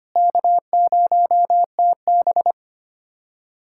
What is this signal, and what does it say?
Morse code, heard as a keyed tone, about 25 wpm, 700 Hz. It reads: K0TB
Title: USA Callsign DE Callsign - 4 Characters - 25wpm